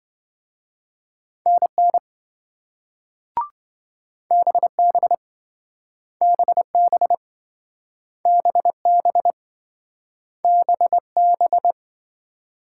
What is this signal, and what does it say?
Morse code, heard as a keyed tone, about 27 wpm, 700 Hz. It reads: NN BB BB BB BB